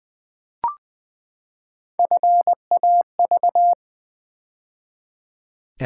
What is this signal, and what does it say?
Morse code, heard as a keyed tone, about 20 wpm, 700 Hz. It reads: FAV
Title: Build A Word - 3 Letters At A Time - 9-Letter Words - 20wpm